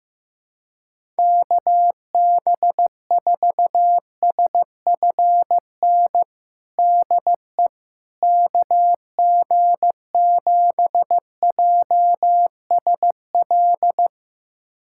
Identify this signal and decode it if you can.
Morse code, heard as a keyed tone, about 15 wpm, 700 Hz. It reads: KB4SFN DE KG7JSL
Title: USA Callsign DE Callsign - 6 Characters - 15wpm